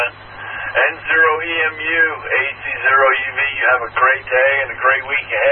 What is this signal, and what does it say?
Emergency Alert System (SAME) tones (header bursts), heard as harsh buzzing bursts, with speech. It is heard from 0.3 to 5.2 s.